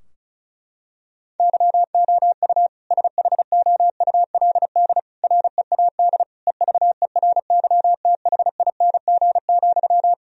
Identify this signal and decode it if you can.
Morse code, heard as a keyed tone, about 36 wpm, 700 Hz. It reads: YOU SHOULD READ EVERYTHING,
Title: Sentences from Top 1000 Words - Review Part 4 - 35wpm